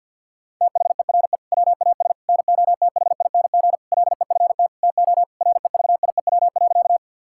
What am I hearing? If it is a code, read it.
Morse code, heard as a keyed tone, about 50 wpm, 700 Hz. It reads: THERE WAS NOTHING LEFT TO REVIEW.
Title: Sentences from Top 1000 Words - Review Part 6 - 50wpm